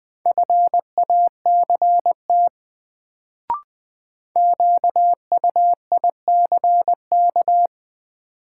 Morse code, heard as a keyed tone, about 20 wpm, 700 Hz. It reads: FACT QUICK
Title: Sentences from Top 500 Words - New Word Drill - 20wpm